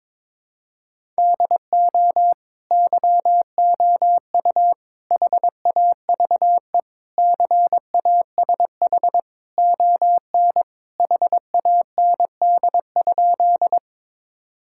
Morse code, heard as a keyed tone, about 22 wpm, 700 Hz. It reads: DO YOU HAVE CASH ON HAND?
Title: Sentences from Top 1000 Words - Review Part 5 - 22wpm